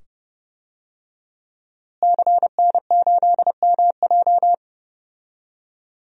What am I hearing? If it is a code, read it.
Morse code, heard as a keyed tone, about 30 wpm, 700 Hz. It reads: CN8MJ
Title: DXCC-100 Call Signs - 5 Characters - Repeated 2x - 30wpm